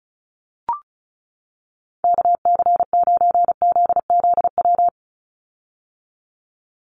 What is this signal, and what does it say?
Morse code, heard as a keyed tone, about 34 wpm, 700 Hz. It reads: KC9ZZW
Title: USA Call Signs 35wpm